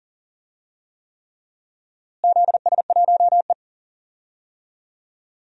Morse code, heard as a keyed tone, about 40 wpm, 700 Hz. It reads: ZS1E
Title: DXCC-100 Call Signs - 4 Characters - Repeated 3x - 40wpm